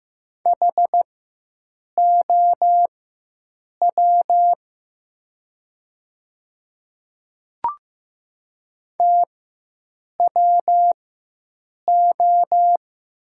Morse code, heard as a keyed tone, about 15 wpm, spaced as for 7 wpm, 700 Hz. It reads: HOW TWO
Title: 4x Character Spacing - Top 100 Words - 15wpm